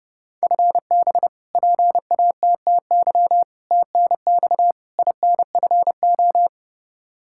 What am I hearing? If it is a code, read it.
Morse code, heard as a keyed tone, about 32 wpm, 700 Hz. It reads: FB PATTY TNX INFO
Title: K1USN SST QSO Practice 30wpm